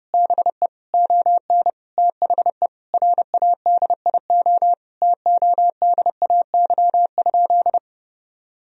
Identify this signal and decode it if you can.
Morse code, heard as a keyed tone, about 30 wpm, 700 Hz. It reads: BE ON THE RADIO TODAY?
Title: Sentences from Top 1000 Words - Review Part 3 - 30wpm